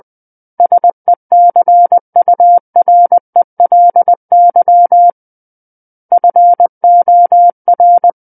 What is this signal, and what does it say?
Morse code, heard as a keyed tone, about 20 wpm, 700 Hz. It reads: SECURELY FOR
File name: 20 WPM